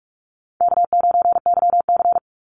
Morse code, heard as a keyed tone, about 44 wpm, 700 Hz. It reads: K9YC
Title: Easy Call Signs Repeated 3x - 45wpm